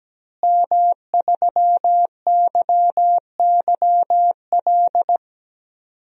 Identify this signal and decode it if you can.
Morse code, heard as a keyed tone, about 17 wpm, 700 Hz. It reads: M3YYL
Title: DXCC-100 Call Signs - 5 Characters - Repeated 2x - 17wpm